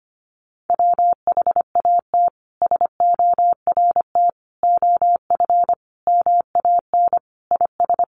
Morse code, heard as a keyed tone, about 25 wpm, 700 Hz. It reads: WHAT SORT OF MAN IS
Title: Sentences from Top 1300 Words 25wpm